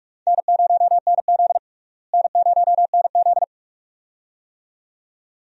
Morse code, heard as a keyed tone, about 46 wpm, 700 Hz. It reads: N0NZ N0NZ
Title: USA Call Signs - 4 Characters - Repeated 2x - 45wpm